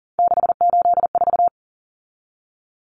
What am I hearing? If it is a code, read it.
Morse code, heard as a keyed tone, about 40 wpm, 700 Hz. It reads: 684
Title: Numbers - 3-Digit 40wpm